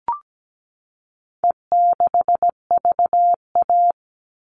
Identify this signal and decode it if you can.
Morse code, heard as a keyed tone, about 17 wpm, 700 Hz. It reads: E6VA
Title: All Cut Numbers - 4-Digit 17wpm